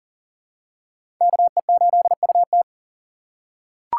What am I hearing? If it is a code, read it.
Morse code, heard as a keyed tone, about 40 wpm, 700 Hz. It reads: KE8UT
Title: USA Call Signs - 2x Character Spacing - 40wpm